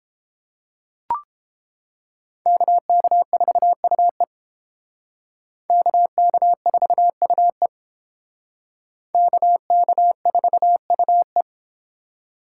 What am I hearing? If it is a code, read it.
Morse code, heard as a keyed tone, about 29 wpm, 700 Hz. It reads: KK4UE KK4UE KK4UE